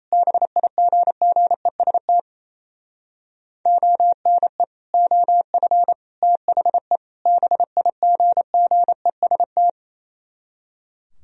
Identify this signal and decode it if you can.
Morse code, heard as a keyed tone, about 34 wpm, 700 Hz. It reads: BIGGEST ONE OF THE BIGGEST
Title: Speed-Racing - Sets of 4 Words 28wpm